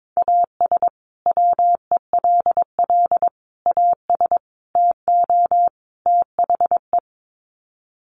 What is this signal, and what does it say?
Morse code, heard as a keyed tone, about 22 wpm, 700 Hz. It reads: AS WELL AS TO THE
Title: Sets of 5 Words Encore2 22wpm